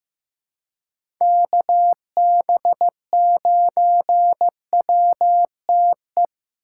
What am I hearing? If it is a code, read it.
Morse code, heard as a keyed tone, about 15 wpm, 700 Hz. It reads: KB9WTE
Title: USA Call Signs - 6 Characters - 4x Character Spacing - 15wpm